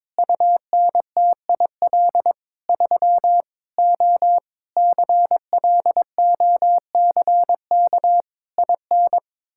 Morse code, heard as a keyed tone, about 22 wpm, 700 Hz. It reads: UNTIL 3 O CLOCK IN
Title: Pangram Challenge v2 - 22wpm